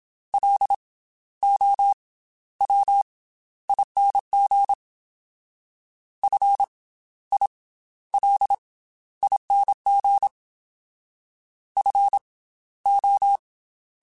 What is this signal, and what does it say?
Morse code, heard as a keyed tone, about 27 wpm, spaced as for 13 wpm, 800 Hz. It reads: LOWING FILING FO
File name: Special INGs / Special INGs-mp3s / Slow ing @25